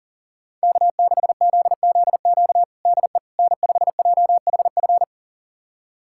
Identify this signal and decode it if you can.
Morse code, heard as a keyed tone, about 40 wpm, 700 Hz. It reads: K6ZZQ DE N5JHF
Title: USA Callsign DE Callsign - 5 Characters - 40wpm